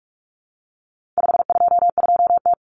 Morse code, heard as a keyed tone, about 44 wpm, 700 Hz. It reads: 522T
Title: Common Cut Numbers - 4-Digit 45wpm